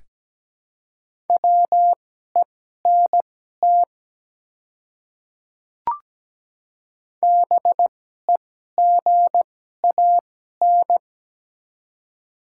Morse code, heard as a keyed tone, about 17 wpm, spaced as for 12 wpm, 700 Hz. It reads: WENT BEGAN
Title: 2x Character Spacing - Top 300 Words - 17wpm